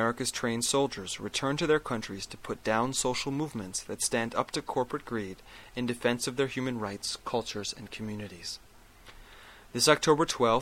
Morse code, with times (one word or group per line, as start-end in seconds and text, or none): none